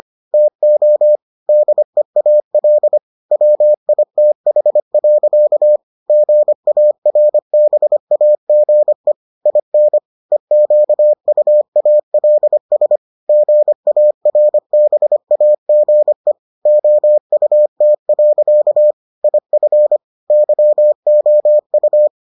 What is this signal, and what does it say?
Morse code, heard as a keyed tone, about 25 wpm, 600 Hz. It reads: TO DEAL WITH. GARBAGE IN EQUALS GARBAGE OUT. IF YOU